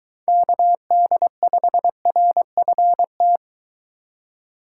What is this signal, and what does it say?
Morse code, heard as a keyed tone, about 23 wpm, 700 Hz. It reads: KD5RFT